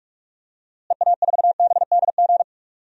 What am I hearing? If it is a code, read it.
Morse code, heard as a keyed tone, about 46 wpm, 700 Hz. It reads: EA4BDG
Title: DXCC-100 Call Signs - 6 Characters - Repeated 3x - 45wpm